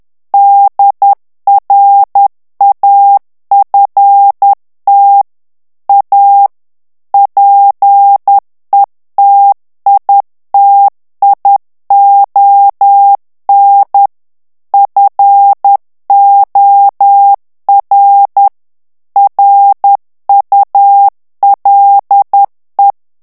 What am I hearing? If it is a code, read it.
Morse code, heard as a keyed tone, about 10 wpm, 800 Hz. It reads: DRAFT A PETITION FOR RULE
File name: ANGLAIS texte clair 3 vitesse 10 mots minute